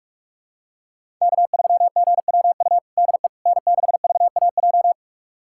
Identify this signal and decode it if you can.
Morse code, heard as a keyed tone, about 42 wpm, 700 Hz. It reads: K3GWU DE N6VAJ
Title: USA Callsign DE Callsign - 5 Characters - 45wpm